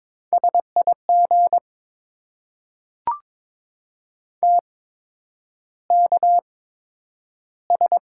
Morse code, heard as a keyed tone, about 21 wpm, spaced as for 9 wpm, 700 Hz. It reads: SIG TKS
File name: Top 90 QSO Elements - 8x Character Spacing - 22wpm